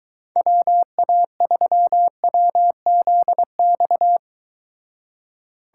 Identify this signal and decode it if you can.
Morse code, heard as a keyed tone, about 23 wpm, 700 Hz. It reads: WA3WZX